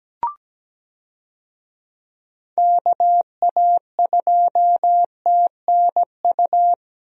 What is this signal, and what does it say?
Morse code, heard as a keyed tone, about 17 wpm, 700 Hz. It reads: KA2TNU
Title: USA Call Signs - 6 Characters - Repeated 2x - 17wpm